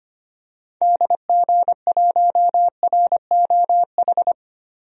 Morse code, heard as a keyed tone, about 25 wpm, 700 Hz. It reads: DG1ROH
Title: DXCC-100 Call Signs - 6 Characters - Repeated 2x - 25wpm